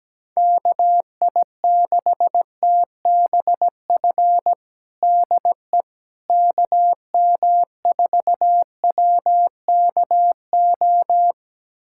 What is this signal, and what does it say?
Morse code, heard as a keyed tone, about 17 wpm, 700 Hz. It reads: KI6TBF DE KM4WKO
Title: USA Callsign DE Callsign 17wpm